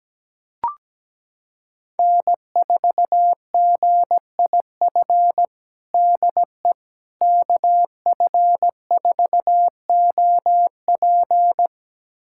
Morse code, heard as a keyed tone, about 17 wpm, 700 Hz. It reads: N4GIF DE KF4OP
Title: USA Callsign DE Callsign - 5 Characters - 17wpm